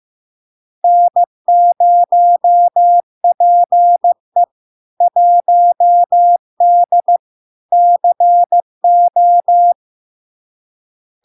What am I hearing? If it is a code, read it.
Morse code, heard as a keyed tone, about 15 wpm, 700 Hz. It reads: N0PE 1D CO
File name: LARK-FD-2022-Log-in-CW-15wpm-700-Hz